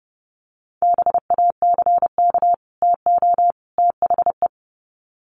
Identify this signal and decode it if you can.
Morse code, heard as a keyed tone, about 30 wpm, 700 Hz. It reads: BACK TO THE
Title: Sets of 3 Words 30wpm